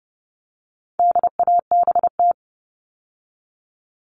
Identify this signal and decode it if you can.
Morse code, heard as a keyed tone, about 29 wpm, 700 Hz. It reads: DABT
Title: All Cut Numbers - 4-Digit 30wpm